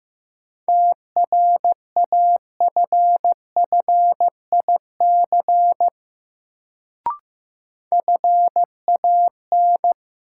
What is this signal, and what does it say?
Morse code, heard as a keyed tone, about 15 wpm, 700 Hz. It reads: TRAFFIC FAN
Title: Sentences from Top 1300 Words - New Word Drill Alt - 15wpm